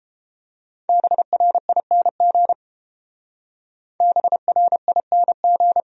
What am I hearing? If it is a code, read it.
Morse code, heard as a keyed tone, about 32 wpm, 700 Hz. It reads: BRING BRING